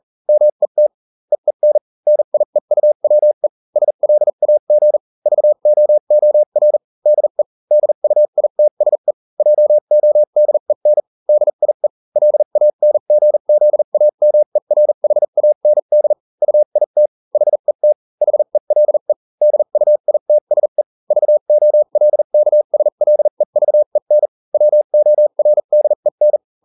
Morse code, heard as a keyed tone, about 40 wpm, spaced as for 35 wpm, 600 Hz. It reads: MET EEN NIEUWE SLAG VOOR DE DUITSE JODEN DIE LANGZAMERHAND UIT HET HELE DUITSE VOLKSLEVEN WORDEN